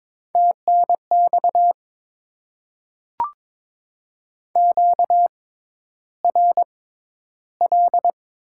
Morse code, heard as a keyed tone, about 22 wpm, spaced as for 11 wpm, 700 Hz. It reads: TNX QRL
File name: Top 20 QSO Elements - 6x Character Spacing - 22wpm